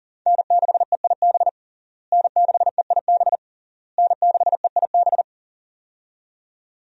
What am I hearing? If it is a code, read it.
Morse code, heard as a keyed tone, about 37 wpm, 700 Hz. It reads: N6EIB N6EIB N6EIB
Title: USA Call Signs - 5 Characters - Repeated 3x - 40wpm